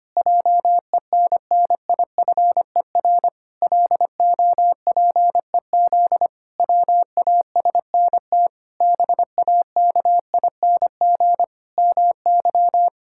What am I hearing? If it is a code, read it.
Morse code, heard as a keyed tone, about 25 wpm, 700 Hz. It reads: JENNIFER LOPEZ WASNT BAKING MY
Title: Pangram Challenge v2 - 25wpm